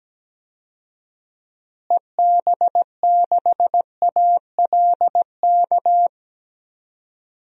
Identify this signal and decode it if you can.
Morse code, heard as a keyed tone, about 17 wpm, 700 Hz. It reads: EB6ALK
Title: DXCC-100 Call Signs - 6 Characters - Repeated 3x - 17wpm